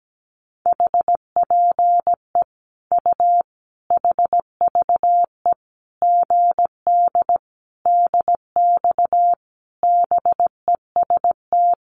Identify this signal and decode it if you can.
Morse code, heard as a keyed tone, about 17 wpm, 700 Hz. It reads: HPE U HVE GD DX BEST
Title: Common QSO Phrases 17wpm